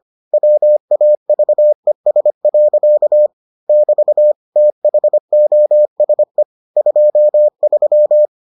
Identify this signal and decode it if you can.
Morse code, heard as a keyed tone, about 25 wpm, 600 Hz. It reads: WAVES. <BT> THOSE 23